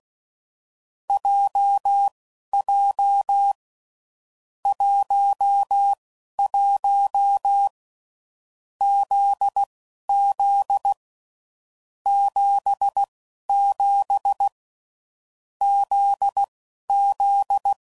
Morse code, heard as a keyed tone, about 16 wpm, spaced as for 12 wpm, 800 Hz. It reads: JJ 11 ZZ 77 ZZ
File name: Groups / Basic Letter Groups / JZ17